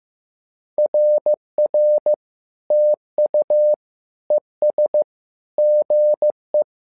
Morse code, heard as a keyed tone, about 15 wpm, 600 Hz. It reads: RR TU ES GE
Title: SOTA QSO Practice 15wpm